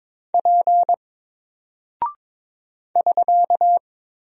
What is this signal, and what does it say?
Morse code, heard as a keyed tone, about 22 wpm, 700 Hz. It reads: P <SK>
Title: Single Character (Letter:Number:Punctuation:Prosign) 22wpm